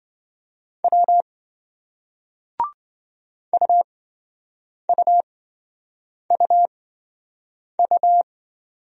Morse code, heard as a keyed tone, about 26 wpm, spaced as for 7 wpm, 700 Hz. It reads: W UUUU